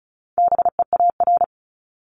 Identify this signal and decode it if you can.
Morse code, heard as a keyed tone, about 34 wpm, 700 Hz. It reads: BEAR
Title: ING Suffix - Top 100 Words with Base Words 35wpm